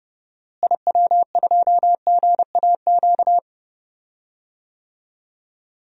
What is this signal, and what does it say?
Morse code, heard as a keyed tone, about 31 wpm, 700 Hz. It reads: IW2GAQ
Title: DXCC-100 Call Signs - 6 Characters - Repeated 3x - 30wpm